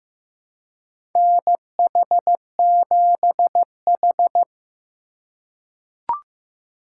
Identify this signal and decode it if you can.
Morse code, heard as a keyed tone, about 15 wpm, 700 Hz. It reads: NH7H
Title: USA Call Signs - 2x Character Spacing - 15wpm